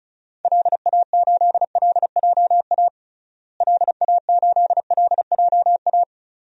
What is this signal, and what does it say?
Morse code, heard as a keyed tone, about 36 wpm, 700 Hz. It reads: LA8LJA LA8LJA
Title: DXCC-100 Call Signs - 6 Characters - Repeated 2x - 35wpm